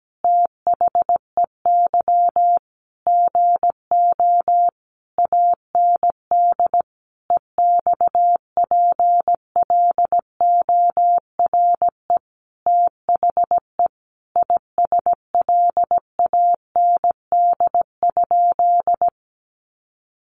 Morse code, heard as a keyed tone, about 17 wpm, 700 Hz. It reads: THEY GO AND EXPLORE THE ISLAND?
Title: Sentences from Top 1100 Words 17wpm